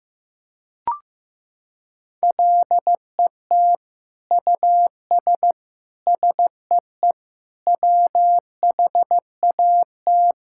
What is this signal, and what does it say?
Morse code, heard as a keyed tone, about 15 wpm, 700 Hz. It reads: LET US SEE WHAT
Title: Sentences from Top 1000 Words - Review Part 5 - 15wpm